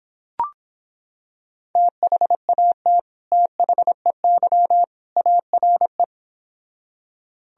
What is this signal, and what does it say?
Morse code, heard as a keyed tone, about 26 wpm, 700 Hz. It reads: THAT THEY ARE